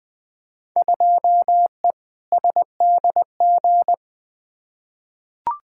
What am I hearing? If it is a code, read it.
Morse code, heard as a keyed tone, about 20 wpm, 700 Hz. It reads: 2E SDG
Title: Field Day Exchanges - 20wpm